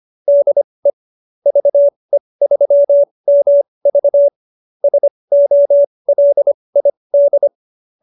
Morse code, heard as a keyed tone, about 25 wpm, spaced as for 20 wpm, 590 Hz. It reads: DE VE3MV SOLID
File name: qso220b 20